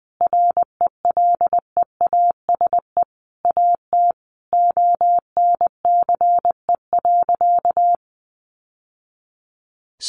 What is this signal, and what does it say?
Morse code, heard as a keyed tone, about 20 wpm, 700 Hz. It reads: RELEASE AT ONCE.
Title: Sentences from Top 1100 Words 20wpm